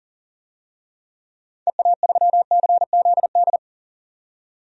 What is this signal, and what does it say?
Morse code, heard as a keyed tone, about 38 wpm, 700 Hz. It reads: EA3CZD
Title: DXCC-100 Call Signs - 6 Characters - 40wpm